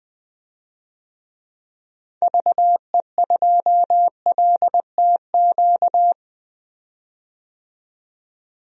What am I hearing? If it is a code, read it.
Morse code, heard as a keyed tone, about 21 wpm, 700 Hz. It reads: VE2LTQ
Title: DXCC-100 Call Signs - 6 Characters - 20wpm